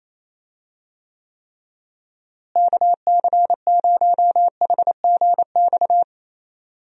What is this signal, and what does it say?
Morse code, heard as a keyed tone, about 28 wpm, 700 Hz. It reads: KC0HGX
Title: USA Call Signs - 6 Characters - Repeated 3x - 28wpm